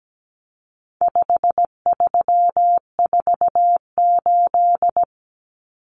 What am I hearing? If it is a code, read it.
Morse code, heard as a keyed tone, about 17 wpm, 700 Hz. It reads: 5348
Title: Common Cut Numbers - 4-Digit 17wpm